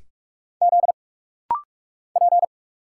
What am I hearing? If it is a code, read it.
Morse code, heard as a keyed tone, about 46 wpm, 700 Hz. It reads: Z P
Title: Single Letter-Number - Rapid-Fire 45wpm